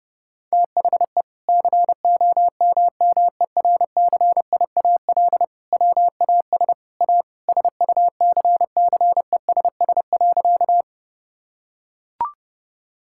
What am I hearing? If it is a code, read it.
Morse code, heard as a keyed tone, about 30 wpm, 700 Hz. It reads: THE COMMERCIAL WAS A SUCCESS.
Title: Sentences from Top 1000 Words - Review Part 5 - 30wpm